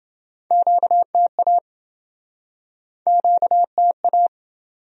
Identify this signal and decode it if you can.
Morse code, heard as a keyed tone, about 30 wpm, 700 Hz. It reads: QTA QTA